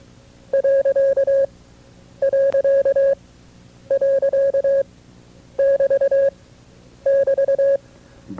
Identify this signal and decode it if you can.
Morse code, heard as a keyed tone, about 23 wpm, spaced as for 9 wpm, 550 Hz. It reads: ...<BT><BT>
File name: FreeFiles / CW-practice / KB6MT-tape2-side2